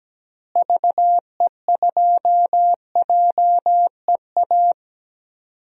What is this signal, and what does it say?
Morse code, heard as a keyed tone, about 17 wpm, 700 Hz. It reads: VE2JEA
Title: DXCC-100 Call Signs - 6 Characters - Repeated 2x - 17wpm